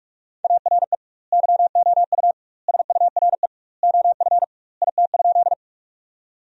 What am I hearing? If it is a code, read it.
Morse code, heard as a keyed tone, about 42 wpm, 700 Hz. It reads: ARE YOU SURE OF IT?
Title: 2x Word-Spacing - Sentences from Top 300 Words - 45wpm